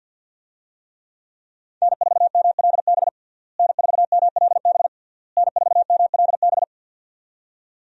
Morse code, heard as a keyed tone, about 46 wpm, 700 Hz. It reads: N4MLB N4MLB N4MLB
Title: USA Call Signs - 5 Characters - Repeated 3x - 50wpm